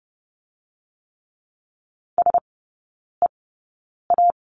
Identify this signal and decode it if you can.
Morse code, heard as a keyed tone, about 30 wpm, spaced as for 9 wpm, 700 Hz. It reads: SEA
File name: 081 Lesson 14e - Number 2 - Three Characters 30wpm